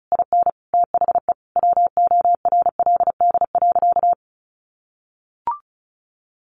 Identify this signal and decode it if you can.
Morse code, heard as a keyed tone, about 34 wpm, 700 Hz. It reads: IN THE WORLD.
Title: Sentences from Top 1300 Words 35wpm